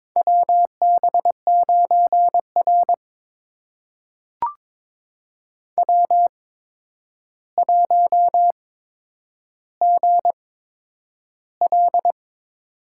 Morse code, heard as a keyed tone, about 22 wpm, spaced as for 9 wpm, 700 Hz. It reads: WB9R W1GL
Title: USA Call Signs - 4 Characters - 8x Character Spacing - 22wpm